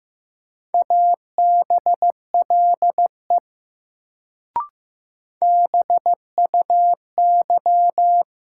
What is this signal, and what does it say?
Morse code, heard as a keyed tone, about 16 wpm, 700 Hz. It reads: ABLE BUY
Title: Sentences from Top 700 Words - New Word Drill - 15wpm